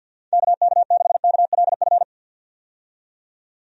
Morse code, heard as a keyed tone, about 51 wpm, 700 Hz. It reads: KK6KLF
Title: USA Call Signs - 6 Characters - 50wpm